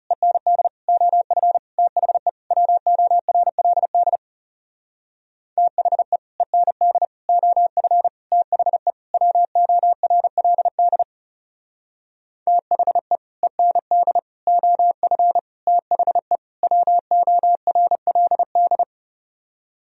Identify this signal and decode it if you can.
Morse code, heard as a keyed tone, about 40 wpm, 700 Hz. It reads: END OF THE WORLD THE END OF THE WORLD THE END OF THE WORLD